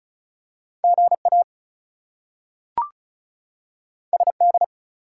Morse code, heard as a keyed tone, about 36 wpm, 700 Hz. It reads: GA SD
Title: US States and Canadian Provinces 35wpm